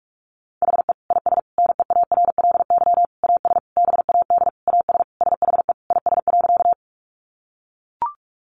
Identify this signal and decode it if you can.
Morse code, heard as a keyed tone, about 44 wpm, 700 Hz. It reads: HE IS NEARLY AS BAD AS SHE IS.
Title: Sentences from Top 1300 Words 45wpm